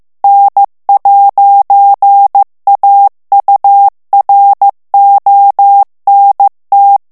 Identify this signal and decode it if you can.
Morse code, heard as a keyed tone, about 15 wpm, 800 Hz. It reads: N'AURONT
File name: morse mp3 / FRANCAIS TEXTE CLAIR 2 VITESSE 14 MOTS MINUTES